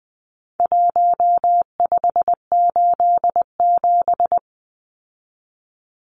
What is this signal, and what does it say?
Morse code, heard as a keyed tone, about 20 wpm, 700 Hz. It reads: 1587
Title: Numbers - Contest Style - 4-Digit 20wpm